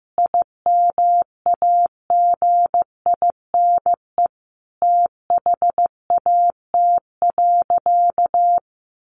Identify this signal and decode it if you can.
Morse code, heard as a keyed tone, about 15 wpm, 700 Hz. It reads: IMAGINE THAT.
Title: Sentences from Top 1400 Words 15wpm